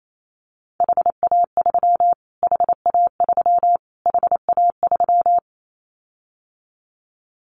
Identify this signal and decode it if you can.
Morse code, heard as a keyed tone, about 28 wpm, 700 Hz. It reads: HA3 HA3 HA3
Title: Callsign Prefixes for Top 25 DXCC - Repeated 3x - 28wpm